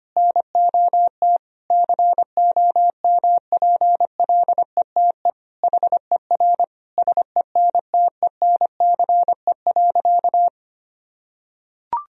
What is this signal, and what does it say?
Morse code, heard as a keyed tone, about 25 wpm, 700 Hz. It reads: NOT COMPLETE HER SENTENCE.
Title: Sentences from Top 1000 Words - Review Part 2 - 25wpm